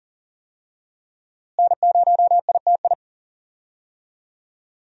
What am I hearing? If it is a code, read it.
Morse code, heard as a keyed tone, about 40 wpm, 700 Hz. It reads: N0ITI
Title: USA Call Signs - 5 Characters - Repeated 2x - 40wpm